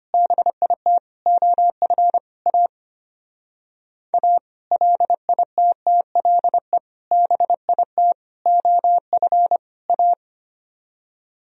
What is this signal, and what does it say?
Morse code, heard as a keyed tone, about 32 wpm, 700 Hz. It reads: BIT OF A A LITTLE BIT OF A